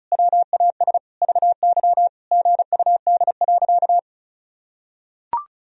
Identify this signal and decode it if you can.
Morse code, heard as a keyed tone, about 36 wpm, 700 Hz. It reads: WAS VY GUD.
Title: Sentences from Top 200 Words and Abbreviations 35wpm